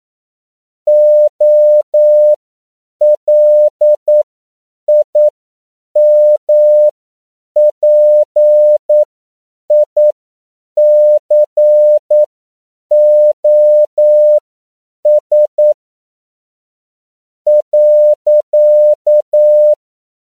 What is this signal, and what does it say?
Morse code, heard as a keyed tone, about 9 wpm, spaced as for 7 wpm, 600 Hz. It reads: OLIMPICOS.